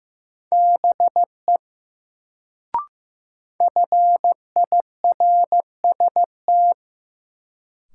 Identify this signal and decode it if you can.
Morse code, heard as a keyed tone, about 15 wpm, 700 Hz. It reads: BE FIRST
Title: Sentences from Top 100 Words - New Word Drill - 15wpm